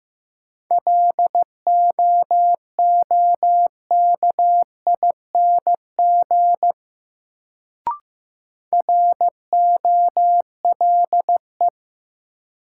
Morse code, heard as a keyed tone, about 16 wpm, 700 Hz. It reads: LOOKING ROLE
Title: Sentences from Top 1000 Words - Review Part 6 - New Word Drill - 15wpm